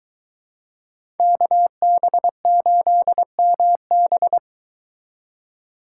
Morse code, heard as a keyed tone, about 23 wpm, 700 Hz. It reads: KB8MB